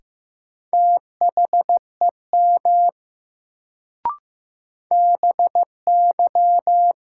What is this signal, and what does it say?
Morse code, heard as a keyed tone, about 15 wpm, 700 Hz. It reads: THEM BY
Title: Sentences from Top 100 Words - New Word Drill - 15wpm